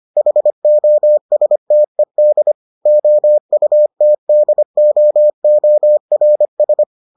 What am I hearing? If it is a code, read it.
Morse code, heard as a keyed tone, about 25 wpm, 600 Hz. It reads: HOSTED OUTDOORS